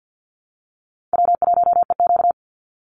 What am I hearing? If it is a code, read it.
Morse code, heard as a keyed tone, about 49 wpm, 700 Hz. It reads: W1EQ
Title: Easy Call Signs Repeated 2x - 50wpm